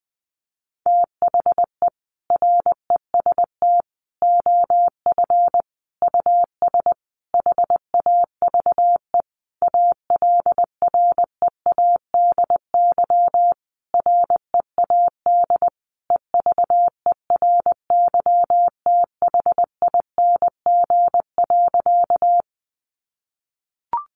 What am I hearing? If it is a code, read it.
Morse code, heard as a keyed tone, about 20 wpm, 700 Hz. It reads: THE REST OF US HAVE ALREADY READ EVERYTHING.
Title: Sentences from Top 700 Words 20wpm